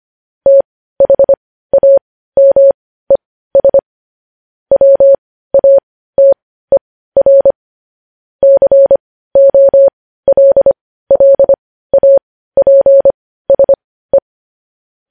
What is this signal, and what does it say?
Morse code, heard as a keyed tone, about 25 wpm, spaced as for 15 wpm, 550 Hz. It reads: THAMES WATER COLLAPSE
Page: news headlines in morse code at 15 wpm